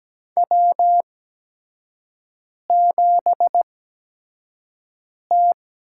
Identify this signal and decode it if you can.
Morse code, heard as a keyed tone, about 17 wpm, spaced as for 5 wpm, 700 Hz. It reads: W7T
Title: USA Call Signs - 5 Characters - 8x Character Spacing - 17wpm